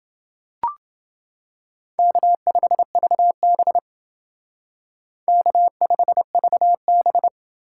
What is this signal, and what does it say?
Morse code, heard as a keyed tone, about 28 wpm, 700 Hz. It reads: K5VB K5VB